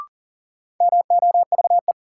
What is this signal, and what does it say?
Morse code, heard as a keyed tone, about 40 wpm, 700 Hz. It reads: MOVE
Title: Sentences from Top 200 Words - New Word Drill - 40wpm